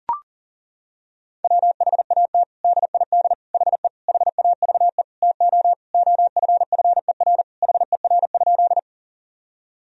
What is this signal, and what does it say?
Morse code, heard as a keyed tone, about 40 wpm, 700 Hz. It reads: WHAT DID HE HAVE TO OFFER HER?
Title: Sentences from Top 1000 Words - Review Part 4 - 40wpm